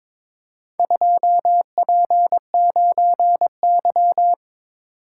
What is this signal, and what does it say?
Morse code, heard as a keyed tone, about 22 wpm, 700 Hz. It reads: 2P9Y
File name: practice / 22wpm / Four Letter-Number Combinations - 22wpm